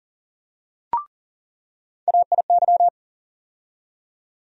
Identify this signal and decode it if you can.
Morse code, heard as a keyed tone, about 37 wpm, 700 Hz. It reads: AIY
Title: USA Call Sign Postfixes - 3 Letters 40wpm